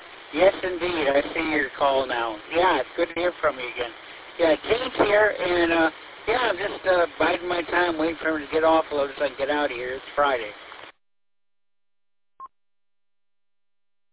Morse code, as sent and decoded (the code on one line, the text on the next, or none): none